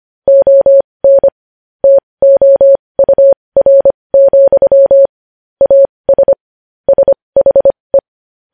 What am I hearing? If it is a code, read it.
Morse code, heard as a keyed tone, about 25 wpm, spaced as for 20 wpm, 550 Hz. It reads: ON TOUR, AS SHE
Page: News headlines in Morse code at 20 wpm